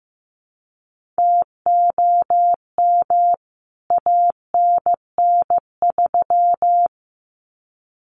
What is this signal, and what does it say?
Morse code, heard as a keyed tone, about 15 wpm, 700 Hz. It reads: TOM ANN3
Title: CWT Exchange 15wpm